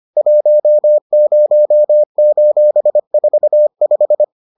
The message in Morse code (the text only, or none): 10:45